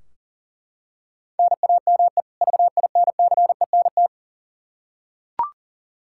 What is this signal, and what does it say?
Morse code, heard as a keyed tone, about 40 wpm, 700 Hz. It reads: NAME VINCENT
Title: Practice QSO - Exchange 1 - Beginning - 40wpm